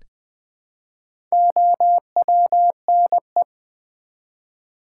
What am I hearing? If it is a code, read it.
Morse code, heard as a keyed tone, about 20 wpm, 700 Hz. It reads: OWNE
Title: Build A Word - 1 Letter At A Time - 5-Letter Words - 20wpm